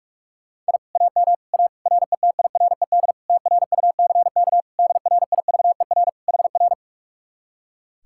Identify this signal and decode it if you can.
Morse code, heard as a keyed tone, about 59 wpm, spaced as for 52 wpm, 700 Hz. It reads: I AM A RETIRED TRUCK DRIVER HR
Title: Speed-Racing Common QSO Phrases - 45wpm